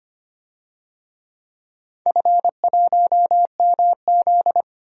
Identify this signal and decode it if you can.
Morse code, heard as a keyed tone, about 25 wpm, 700 Hz. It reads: F1MZ
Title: DXCC-100 Call Signs - 4 Characters - Repeated 3x - 25wpm